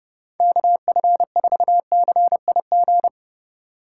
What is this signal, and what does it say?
Morse code, heard as a keyed tone, about 37 wpm, 700 Hz. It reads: KF4CIG
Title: Speed Racing - USA 2x3 Call Signs 30wpm